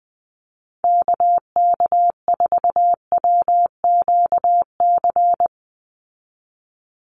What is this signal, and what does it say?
Morse code, heard as a keyed tone, about 20 wpm, 700 Hz. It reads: KK4WQC
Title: USA Call Signs 20wpm